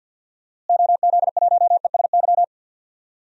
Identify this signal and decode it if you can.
Morse code, heard as a keyed tone, about 51 wpm, 700 Hz. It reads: OZ1ESY
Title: DXCC-100 Call Signs - 6 Characters - Repeated 2x - 50wpm